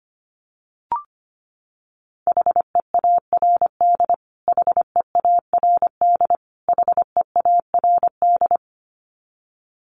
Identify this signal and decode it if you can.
Morse code, heard as a keyed tone, about 25 wpm, 700 Hz. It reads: HEARD HEARD HEARD
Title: Top 500 Words Repeated 25wpm